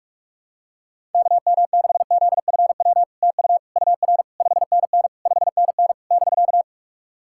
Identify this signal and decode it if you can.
Morse code, heard as a keyed tone, about 44 wpm, 700 Hz. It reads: KM6ZFW TU UR 5NN 5NN <BK>
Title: SOTA QSO Practice 45wpm